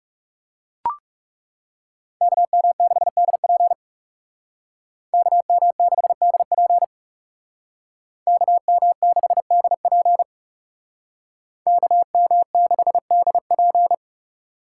Morse code, heard as a keyed tone, about 37 wpm, 700 Hz. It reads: KM6DP KM6DP KM6DP KM6DP
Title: Speed Racing - USA 2x2 Call Signs 30wpm